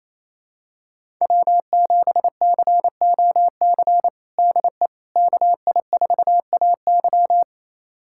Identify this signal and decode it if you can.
Morse code, heard as a keyed tone, about 28 wpm, 700 Hz. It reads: W7COC DE KI4AY
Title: USA Callsign DE Callsign - 5 Characters - 28wpm